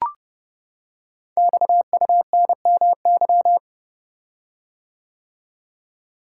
Morse code, heard as a keyed tone, about 30 wpm, 700 Hz. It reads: XUNMY
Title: Code Groups - 5 Letters - 30wpm